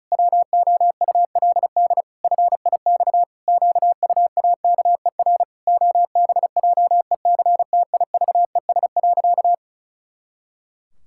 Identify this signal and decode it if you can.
Morse code, heard as a keyed tone, about 34 wpm, 700 Hz. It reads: WOULD FIX QUAKER OBJECTIVES.
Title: Pangram Challenge v2 - 35wpm